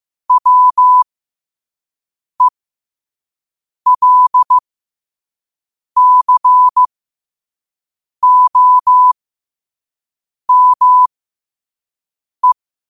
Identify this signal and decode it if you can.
Morse code, heard as a keyed tone, about 15 wpm, spaced as for 5 wpm, 1000 Hz. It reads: WELCOME
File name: qotdfiles / 5wpm